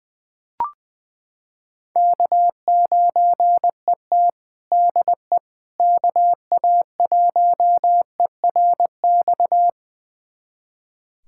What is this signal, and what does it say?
Morse code, heard as a keyed tone, about 20 wpm, 700 Hz. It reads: K9ET DE KA1ERX
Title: USA Callsign DE Callsign 20wpm